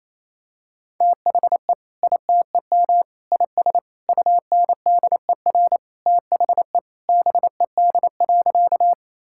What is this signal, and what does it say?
Morse code, heard as a keyed tone, about 28 wpm, 700 Hz. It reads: THE ITEM IS UNDER THE BED.
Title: Sentences from Top 1000 Words - Review Part 6 - 28wpm